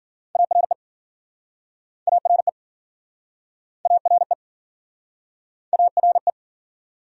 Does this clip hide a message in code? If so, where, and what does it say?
Yes: Morse code from 0.4 to 6.3 s, heard as a keyed tone, about 52 wpm, 700 Hz. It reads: ARE ARE ARE ARE